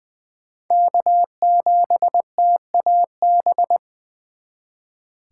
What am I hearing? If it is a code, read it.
Morse code, heard as a keyed tone, about 20 wpm, 700 Hz. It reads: K7TAB
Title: 2023 SOTA Chaser Call Signs - 20wpm